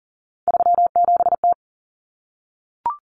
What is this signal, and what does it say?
Morse code, heard as a keyed tone, about 40 wpm, 700 Hz. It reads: 37T
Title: Common Cut Numbers - 3-Digit 40wpm